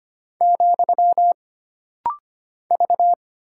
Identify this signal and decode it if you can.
Morse code, heard as a keyed tone, about 25 wpm, 700 Hz. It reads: , V
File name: Single Character (Letter:Number:Punctuation:Prosign) - Rapid-Fire - 25wpm